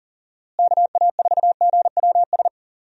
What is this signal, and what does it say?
Morse code, heard as a keyed tone, about 40 wpm, 700 Hz. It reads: KA4GWS
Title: USA Call Signs - 6 Characters - Repeated 2x - 40wpm